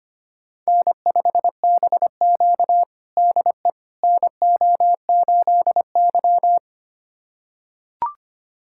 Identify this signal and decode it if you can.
Morse code, heard as a keyed tone, about 25 wpm, 700 Hz. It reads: N5BQ DE NO8Y
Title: USA Callsign DE Callsign - 4 Characters - 25wpm